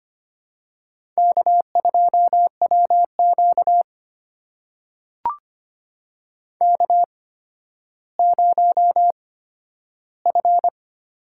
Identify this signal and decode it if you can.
Morse code, heard as a keyed tone, about 25 wpm, spaced as for 12 wpm, 700 Hz. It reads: K2WQ K0F
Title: USA Call Signs - 8x Character Spacing - 25wpm